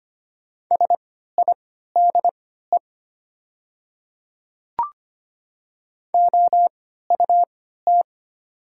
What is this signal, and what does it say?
Morse code, heard as a keyed tone, about 25 wpm, spaced as for 14 wpm, 700 Hz. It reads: SIDE OUT
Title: 3x Character Spacing - Top 100 Words - 25wpm